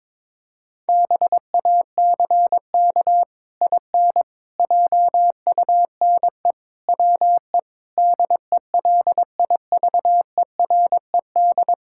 Morse code, heard as a keyed tone, about 22 wpm, 700 Hz. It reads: BACK IN JUNE WE DELIVERED
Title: Pangram Challenge v2 - 22wpm